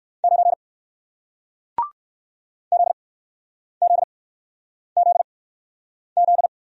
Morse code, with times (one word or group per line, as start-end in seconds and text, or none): 0.2-0.5 9
2.7-6.5 ZZZZ